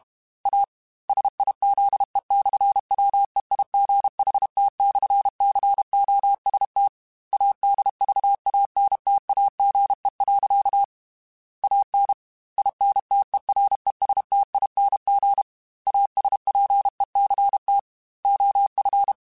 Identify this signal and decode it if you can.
Morse code, heard as a keyed tone, about 32 wpm, 800 Hz. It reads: A SIZE/WEIGHT/COST ADVANTAGE. AN INTERESTING ASPECT OF
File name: Technical Subjects / Technical Subjects-mp3s / Sw Pwr Sup @30